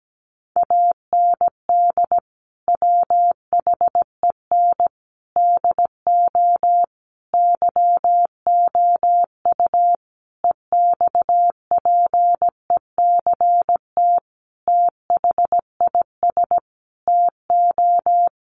Morse code, heard as a keyed tone, about 17 wpm, 700 Hz. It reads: AND WHEN DO YOU EXPECT THIS TO
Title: Sentences from Top 1100 Words 17wpm